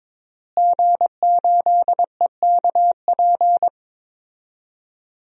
Morse code, heard as a keyed tone, about 22 wpm, 700 Hz. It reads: G8EKP
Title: DXCC-100 Call Signs - 5 Characters - Repeated 3x - 22wpm